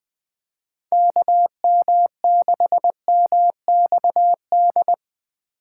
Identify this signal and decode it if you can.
Morse code, heard as a keyed tone, about 20 wpm, 700 Hz. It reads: KM6MXD
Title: USA Call Signs - 6 Characters - 2x Character Spacing - 20wpm